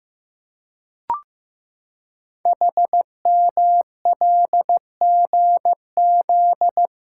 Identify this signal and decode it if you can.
Morse code, heard as a keyed tone, about 15 wpm, 700 Hz. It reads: HMLGZ
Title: Code Groups - 5 Letters - 15wpm